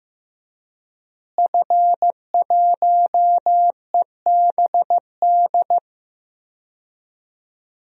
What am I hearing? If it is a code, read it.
Morse code, heard as a keyed tone, about 15 wpm, 700 Hz. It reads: F1EBD
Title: DXCC-100 Call Signs - 5 Characters - Repeated 3x - 15wpm